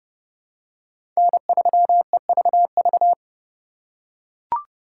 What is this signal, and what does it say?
Morse code, heard as a keyed tone, about 32 wpm, 700 Hz. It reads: N3EVV
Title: USA Call Signs - 5 Characters - 2x Character Spacing - 30wpm